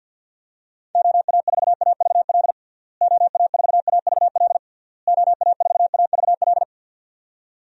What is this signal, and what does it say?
Morse code, heard as a keyed tone, about 52 wpm, 700 Hz. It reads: OA4AVL OA4AVL OA4AVL
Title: USA Call Signs - 6 Characters - Repeated 3x - 50wpm